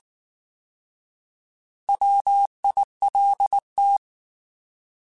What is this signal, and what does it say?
Morse code, heard as a keyed tone, about 19 wpm, 800 Hz. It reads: WILT
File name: Alphabetical / Alphabetical mp3s / Ws @18